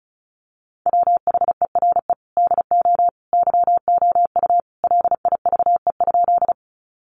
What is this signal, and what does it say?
Morse code, heard as a keyed tone, about 34 wpm, 700 Hz. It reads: WHERE DO YOU LIVE?
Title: Sentences from Top 200 Words 35wpm